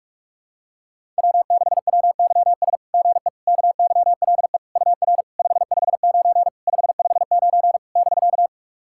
Morse code, heard as a keyed tone, about 44 wpm, 690 Hz. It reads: W6WYS GE KYLE UR 559 559 <BK>
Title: SOTA QSO Practice 45wpm